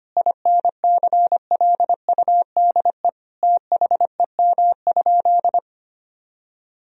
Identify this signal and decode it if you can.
Morse code, heard as a keyed tone, about 25 wpm, 700 Hz. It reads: INCLUDE THEM?
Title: Sentences from Top 1000 Words - Review Part 3 - 25wpm